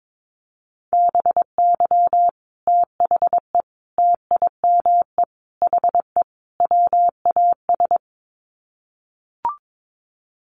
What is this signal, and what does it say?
Morse code, heard as a keyed tone, about 22 wpm, 700 Hz. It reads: BY THE TIME HE WAS
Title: Sets of 5 Words Encore 22wpm